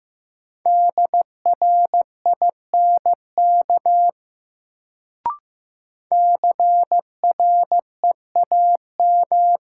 Morse code, heard as a keyed tone, about 15 wpm, 700 Hz. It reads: DRINK CREAM
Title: Sentences from Top 1500 Words - New Word Drill - 15wpm